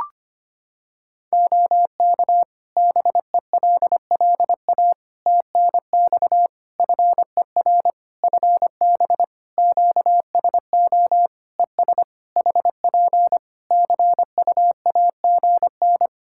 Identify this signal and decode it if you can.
Morse code, heard as a keyed tone, about 25 wpm, 700 Hz. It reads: OK BELLA TNX FER FB QSO ES HP CUAGN
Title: K1USN SST QSO Practice 25wpm